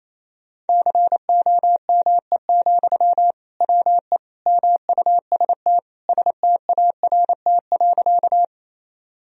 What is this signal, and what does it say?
Morse code, heard as a keyed tone, about 28 wpm, 700 Hz. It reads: COME, WE MUST START.
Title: Sentences from Top 1000 Words - Review Part 1 - 28wpm